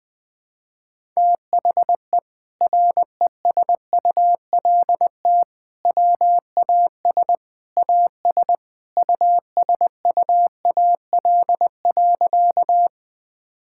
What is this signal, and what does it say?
Morse code, heard as a keyed tone, about 20 wpm, 700 Hz. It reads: THE RESULT WAS AS USUAL.
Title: Sentences from Top 1000 Words - Review Part 5 - 20wpm